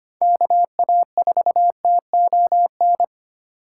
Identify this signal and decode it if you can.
Morse code, heard as a keyed tone, about 25 wpm, 700 Hz. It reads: KA4TON
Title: USA Call Signs - 6 Characters - 3x Character Spacing - 25wpm